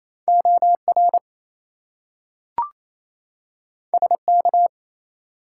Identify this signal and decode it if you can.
Morse code, heard as a keyed tone, about 28 wpm, 700 Hz. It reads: OR SK
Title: US States and Canadian Provinces 28wpm